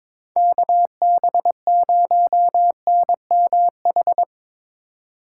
Morse code, heard as a keyed tone, about 22 wpm, 700 Hz. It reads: KB0NMH
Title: Speed Racing - USA 2x3 Call Signs 22wpm